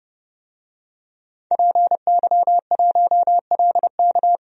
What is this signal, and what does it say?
Morse code, heard as a keyed tone, about 30 wpm, 700 Hz. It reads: PY1LK
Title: DXCC-100 Call Signs - 5 Characters - 30wpm